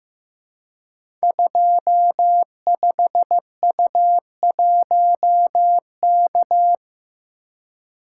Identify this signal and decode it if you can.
Morse code, heard as a keyed tone, about 15 wpm, 700 Hz. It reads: 25U1K
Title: Code Groups - 5 Letter:Numbers - 15wpm